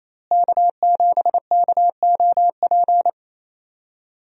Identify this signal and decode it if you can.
Morse code, heard as a keyed tone, about 28 wpm, 700 Hz. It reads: K7KOP
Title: USA Call Signs - 5 Characters - Repeated 2x - 28wpm